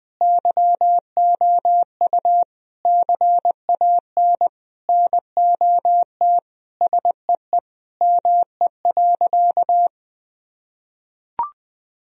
Morse code, heard as a keyed tone, about 20 wpm, 700 Hz. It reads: YOU CAN NOT SEE ME.
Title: Sentences from Top 1000 Words - Review Part 3 - 20wpm